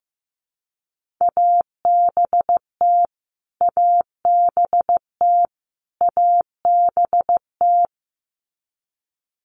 Morse code, heard as a keyed tone, about 15 wpm, 700 Hz. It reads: ABT ABT ABT
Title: Abbreviations Repeated 3x - 15wpm